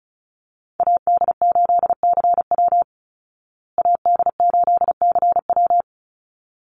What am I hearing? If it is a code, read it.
Morse code, heard as a keyed tone, about 34 wpm, 700 Hz. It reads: AD8CW AD8CW
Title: Call Signs Repeated 3x - 35wpm